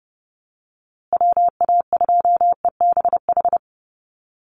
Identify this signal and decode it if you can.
Morse code, heard as a keyed tone, about 30 wpm, 700 Hz. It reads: WA2EBH
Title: USA Call Signs Repeated 2x 30wpm